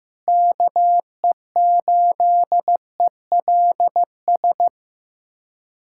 Morse code, heard as a keyed tone, about 16 wpm, 700 Hz. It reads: KE8ELS
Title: USA Callsign DE Callsign - 6 Characters - 15wpm